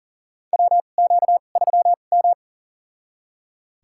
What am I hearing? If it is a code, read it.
Morse code, heard as a keyed tone, about 40 wpm, spaced as for 29 wpm, 700 Hz. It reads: WQ3M
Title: USA Call Signs - 4 Characters - 2x Character Spacing - 40wpm